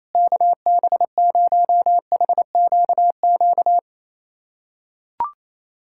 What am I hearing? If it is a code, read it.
Morse code, heard as a keyed tone, about 27 wpm, 700 Hz. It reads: KB0HQQ
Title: USA Callsign DE Callsign - 6 Characters - 28wpm